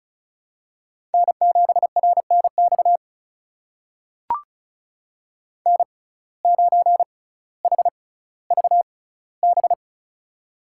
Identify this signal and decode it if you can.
Morse code, heard as a keyed tone, about 36 wpm, spaced as for 18 wpm, 700 Hz. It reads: N7RNX N9HVB
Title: USA Call Signs - 5 Characters - 6x Character Spacing - 35wpm